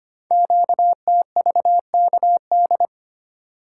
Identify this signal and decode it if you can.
Morse code, heard as a keyed tone, about 25 wpm, 700 Hz. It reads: QTVKD
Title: Code Groups - 5 Letter:Numbers - 25wpm